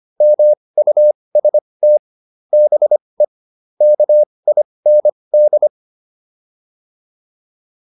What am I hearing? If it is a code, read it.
Morse code, heard as a keyed tone, about 25 wpm, spaced as for 20 wpm, 600 Hz. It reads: MUST BE KIND
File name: PR301 20